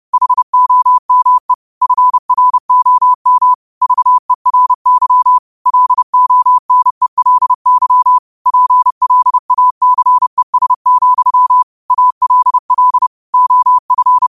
Morse code, heard as a keyed tone, about 30 wpm, 1000 Hz. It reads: HOME FROM VERY LONELY PLACES, ALL OF
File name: qotdfiles / 30wpm